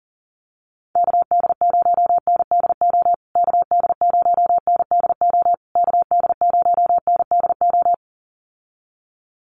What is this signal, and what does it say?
Morse code, heard as a keyed tone, about 40 wpm, 700 Hz. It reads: KD0NDO KD0NDO KD0NDO
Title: USA Call Signs Repeated 3x 40wpm